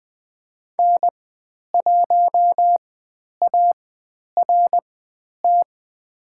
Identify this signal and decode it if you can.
Morse code, heard as a keyed tone, about 20 wpm, spaced as for 10 wpm, 700 Hz. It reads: N1ART
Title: 102 Lesson 17g - Letter u - Call Signs 20wpm